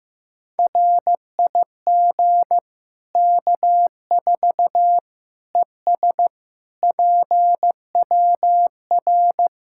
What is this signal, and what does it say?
Morse code, heard as a keyed tone, about 16 wpm, 700 Hz. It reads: RIG K4 ES PWR
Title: Practice QSO - Exchange 2 - Info - 15wpm